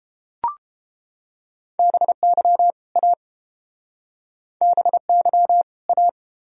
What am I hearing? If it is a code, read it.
Morse code, heard as a keyed tone, about 30 wpm, 700 Hz. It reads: BY A BY A